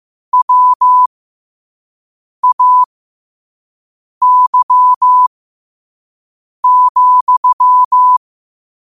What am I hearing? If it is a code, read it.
Morse code, heard as a keyed tone, about 15 wpm, spaced as for 5 wpm, 1000 Hz. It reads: WAY,
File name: qotdfiles / 5wpm